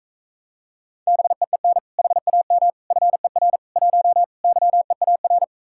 Morse code, heard as a keyed tone, about 43 wpm, 700 Hz. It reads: BEEN HAM FER 1 YEAR